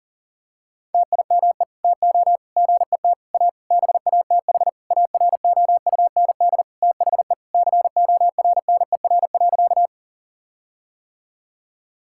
Morse code, heard as a keyed tone, about 40 wpm, 700 Hz. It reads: TIME TO GET A BATH AROUND THE CORNER.
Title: Sentences from Top 1600 Words 40wpm